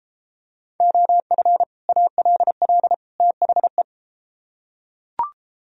Morse code, heard as a keyed tone, about 30 wpm, 700 Hz. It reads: OF ALL THE